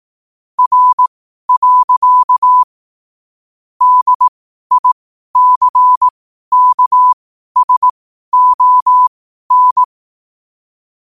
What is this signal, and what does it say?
Morse code, heard as a keyed tone, about 18 wpm, spaced as for 12 wpm, 1000 Hz. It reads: R. DICKSON
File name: qotdfiles / 12wpm